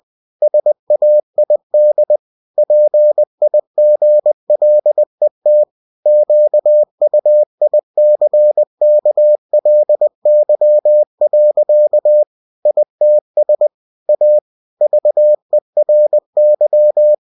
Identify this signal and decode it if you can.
Morse code, heard as a keyed tone, about 20 wpm, 600 Hz. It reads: SAID PIGLET QUICKLY. ITS A VERY
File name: cw-books / winnie-the-pooh / winnie-20wpm-page-0024